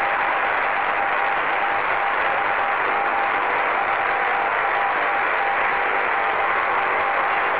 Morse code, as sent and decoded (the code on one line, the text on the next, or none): none